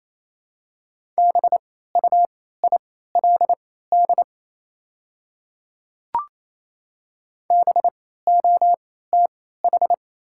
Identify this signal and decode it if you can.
Morse code, heard as a keyed tone, about 28 wpm, spaced as for 16 wpm, 700 Hz. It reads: BUILD BOTH
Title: 3x Character Spacing - Top 300 Words - 28wpm